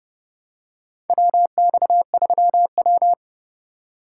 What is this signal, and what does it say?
Morse code, heard as a keyed tone, about 30 wpm, 700 Hz. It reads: WX3W
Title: USA Call Signs - 4 Characters - 3x Character Spacing - 30wpm